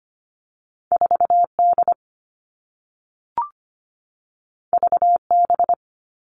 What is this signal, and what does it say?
Morse code, heard as a keyed tone, about 25 wpm, 700 Hz. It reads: 4D VB
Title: All Cut Numbers - 2-Digit 25wpm